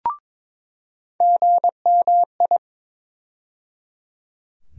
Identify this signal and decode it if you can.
Morse code, heard as a keyed tone, about 22 wpm, 700 Hz. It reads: GMI
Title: USA Call Sign Postfixes - 3 Letters 22wpm